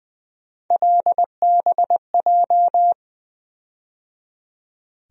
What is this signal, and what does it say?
Morse code, heard as a keyed tone, about 20 wpm, 700 Hz. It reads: LBJ
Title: USA Call Sign Postfixes - 3 Letters 20wpm